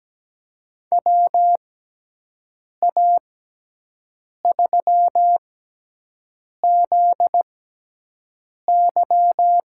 Morse code, heard as a keyed tone, about 17 wpm, spaced as for 6 wpm, 700 Hz. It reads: WA3ZY
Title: USA Call Signs - 6 Characters - 6x Character Spacing - 17wpm